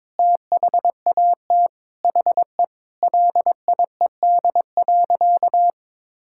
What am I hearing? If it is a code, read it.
Morse code, heard as a keyed tone, about 21 wpm, 700 Hz. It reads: THAT HE LIED.
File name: Sentences with 7-Word Phrases - 22wpm